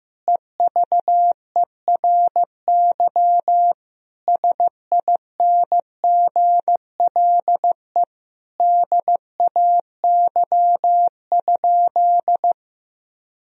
Morse code, heard as a keyed tone, about 16 wpm, 700 Hz. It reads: EVERY SINGLE DAY?
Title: Sentences from Top 1000 Words - Review Part 4 - 15wpm